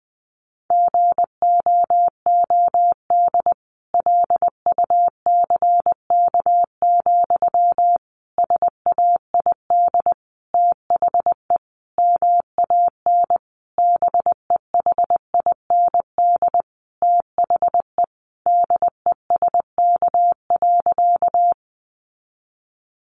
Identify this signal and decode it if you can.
Morse code, heard as a keyed tone, about 20 wpm, 700 Hz. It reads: GOOD LUCK, SAID THE MAN BEHIND THE DESK.
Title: A Sound Of Thunder 20wpm